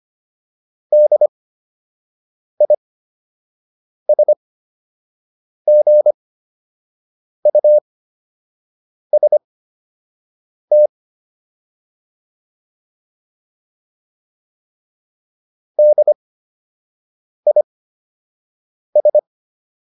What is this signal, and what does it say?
Morse code, heard as a keyed tone, about 25 wpm, spaced as for 6 wpm, 600 Hz. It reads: DISGUST DIS